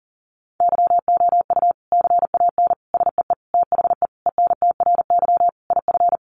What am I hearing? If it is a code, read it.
Morse code, heard as a keyed tone, about 40 wpm, 700 Hz. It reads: YOU CAN SEE THE ENTRY IF
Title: Sentences from Top 1500 Words 40wpm